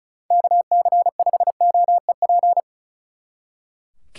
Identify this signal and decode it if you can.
Morse code, heard as a keyed tone, about 44 wpm, 700 Hz. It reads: KC5OEP